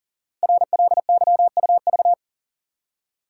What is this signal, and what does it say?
Morse code, heard as a keyed tone, about 42 wpm, 690 Hz. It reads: RLYUV
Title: Code Groups - 5 Letters - 40wpm